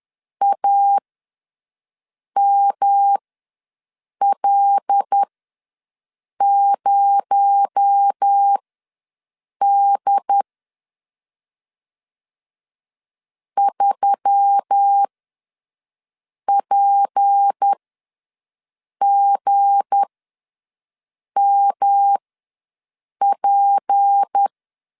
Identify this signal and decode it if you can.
Morse code, heard as a keyed tone, about 10 wpm, spaced as for 5 wpm, 800 Hz. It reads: AML0D 3PGMP
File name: LEZIONI / 7 GF9Z / 7 G ws 10wpm 3843